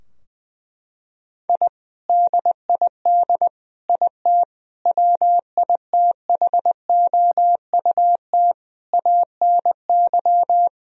Morse code, heard as a keyed tone, about 21 wpm, 700 Hz. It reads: I DID IT WITHOUT ANY
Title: Sentences from Top 1000 Words - Review Part 3 - 20wpm